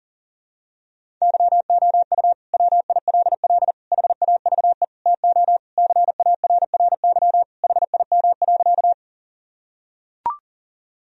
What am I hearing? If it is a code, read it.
Morse code, heard as a keyed tone, about 40 wpm, 700 Hz. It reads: YOU WILL HAVE TO CARRY HIM.
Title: Sentences from Top 1000 Words - Review Part 1 - 40wpm